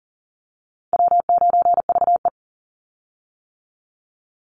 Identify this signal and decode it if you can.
Morse code, heard as a keyed tone, about 40 wpm, 700 Hz. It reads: W9VE
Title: Call Signs - 40wpm